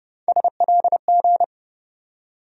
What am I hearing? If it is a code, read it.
Morse code, heard as a keyed tone, about 30 wpm, 700 Hz. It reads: SLG
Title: USA Call Sign Postfixes - 3 Letters 30wpm